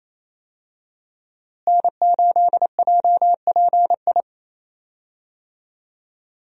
Morse code, heard as a keyed tone, about 28 wpm, 700 Hz. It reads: N8JPI
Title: USA Call Signs - 5 Characters - Repeated 2x - 28wpm